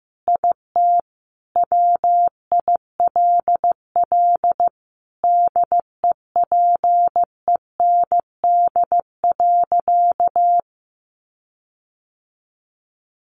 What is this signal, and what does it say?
Morse code, heard as a keyed tone, about 15 wpm, 700 Hz. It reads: IT WILL DEPEND.
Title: Sentences from Top 1300 Words 15wpm